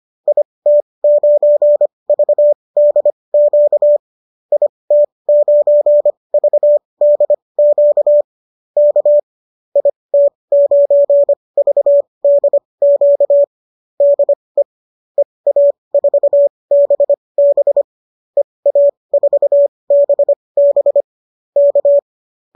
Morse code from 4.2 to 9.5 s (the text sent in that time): IT9VDQ K